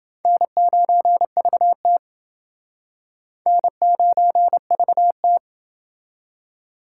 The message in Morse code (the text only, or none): N9VT N9VT